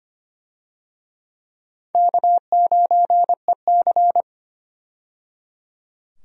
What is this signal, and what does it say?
Morse code, heard as a keyed tone, about 25 wpm, 700 Hz. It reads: K9EC
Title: USA Call Signs - 4 Characters - 25wpm